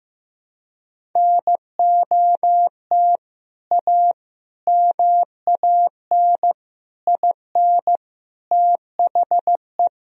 Morse code, heard as a keyed tone, about 15 wpm, 700 Hz. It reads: NOT A MAN IN THE
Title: Sentences from Top 1000 Words - Review Part 1 - 15wpm